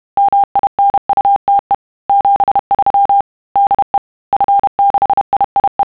Morse code, heard as a keyed tone, about 31 wpm, 800 Hz. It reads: MINUTE 73 DE F6IIE
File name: sons / TEST30